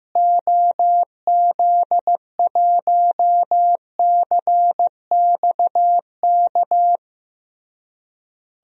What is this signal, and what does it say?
Morse code, heard as a keyed tone, about 15 wpm, 700 Hz. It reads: OZ1CXK
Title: DXCC-100 Call Signs - 6 Characters - Repeated 3x - 15wpm